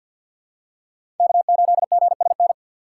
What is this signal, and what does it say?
Morse code, heard as a keyed tone, about 48 wpm, 700 Hz. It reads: K8GSN
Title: USA Call Signs - 5 Characters - 6x Character Spacing - 50wpm